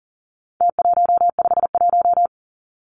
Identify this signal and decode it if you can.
Morse code, heard as a keyed tone, about 40 wpm, 700 Hz. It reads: T151
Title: Numbers - Contest Style - 4-Digit 40wpm